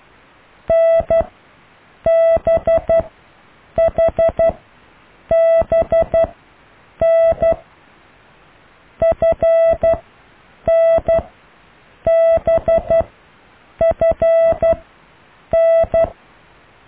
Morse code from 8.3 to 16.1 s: FNBFN